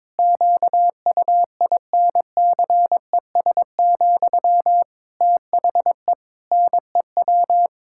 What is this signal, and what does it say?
Morse code, heard as a keyed tone, about 22 wpm, 700 Hz. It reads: QUINCES, THE NEW
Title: Pangram Challenge v2 - 22wpm